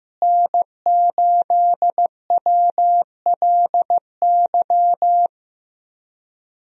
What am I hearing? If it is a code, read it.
Morse code, heard as a keyed tone, about 15 wpm, 700 Hz. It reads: N8WLY
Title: USA Call Signs - 5 Characters - Repeated 3x - 15wpm